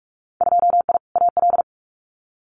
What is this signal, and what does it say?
Morse code, heard as a keyed tone, about 44 wpm, 700 Hz. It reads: 2I AL
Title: Winter Field Day Exchanges 45wpm